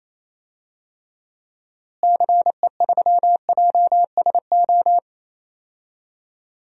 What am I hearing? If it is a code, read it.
Morse code, heard as a keyed tone, about 28 wpm, 700 Hz. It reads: CE3JSO
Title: DXCC-100 Call Signs - 6 Characters - Repeated 2x - 28wpm